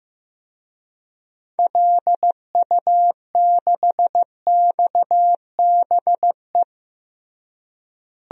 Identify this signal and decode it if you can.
Morse code, heard as a keyed tone, about 15 wpm, 700 Hz. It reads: LU6XBE
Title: DXCC-100 Call Signs - 6 Characters - Repeated 2x - 15wpm